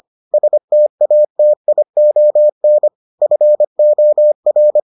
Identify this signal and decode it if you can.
Morse code, heard as a keyed tone, about 25 wpm, 600 Hz. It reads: STATION FOR